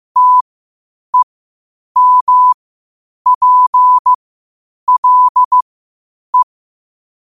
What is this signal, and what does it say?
Morse code, heard as a keyed tone, about 15 wpm, spaced as for 8 wpm, 1000 Hz. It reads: TEMPLE
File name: qotdfiles / 8wpm